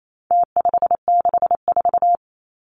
Morse code, heard as a keyed tone, about 28 wpm, 700 Hz. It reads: T564
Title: Numbers - Contest Style - 4-Digit 28wpm